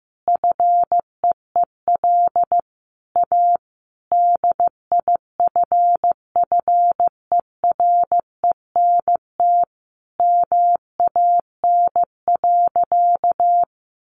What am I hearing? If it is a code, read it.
Morse code, heard as a keyed tone, about 15 wpm, 700 Hz. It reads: FEEL A DIFFERENT MAN.
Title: Sentences from Top 600 Words 15wpm